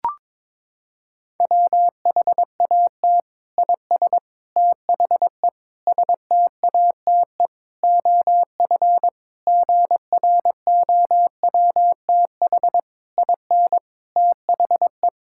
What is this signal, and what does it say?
Morse code, heard as a keyed tone, about 22 wpm, 700 Hz. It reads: WHAT IS THE STATE OF GROWTH IN THE
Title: Sentences from Top 1000 Words - Review Part 5 - 22wpm